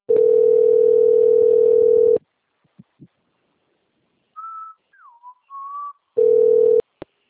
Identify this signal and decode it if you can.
Ringback tone (North America), 440 + 480 Hz.
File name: out- - - - - .34152